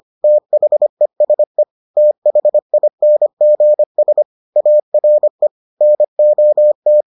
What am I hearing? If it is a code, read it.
Morse code, heard as a keyed tone, about 25 wpm, 600 Hz. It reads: THESE THINGS ARE NOT